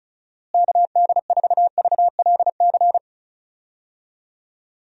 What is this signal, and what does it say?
Morse code, heard as a keyed tone, about 34 wpm, 700 Hz. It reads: KD4VLC
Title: USA Call Signs - 6 Characters - Repeated 2x - 35wpm